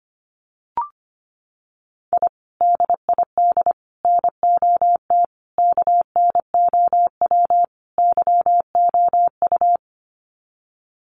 Morse code, heard as a keyed tone, about 25 wpm, 700 Hz. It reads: I DID NOT KNOW YOU
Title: Sets of 5 Words Encore2 25wpm